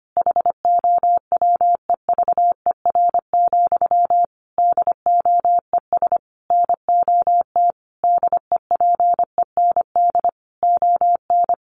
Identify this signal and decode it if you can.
Morse code, heard as a keyed tone, about 25 wpm, 700 Hz. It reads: HOWEVER, DOES NOT DEPEND ON
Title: Sentences from Top 1300 Words 25wpm